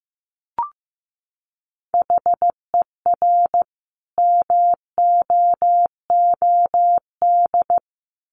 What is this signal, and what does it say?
Morse code, heard as a keyed tone, about 15 wpm, 700 Hz. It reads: HER MOOD
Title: Sentences from Top 1100 Words 15wpm